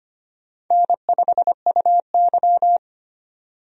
Morse code, heard as a keyed tone, about 25 wpm, 700 Hz. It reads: N5UY
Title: USA Call Signs - 4 Characters - 25wpm